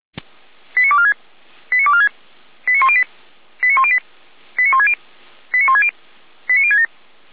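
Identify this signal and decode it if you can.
Five-tone selective call (ZVEI), heard as quick runs of five tones, tones of about 70 ms. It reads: 89246 89246 89108 89108 89160 89160 89067
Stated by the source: Five-tone ZVEI selective call 89072 received by SDR, channel 455.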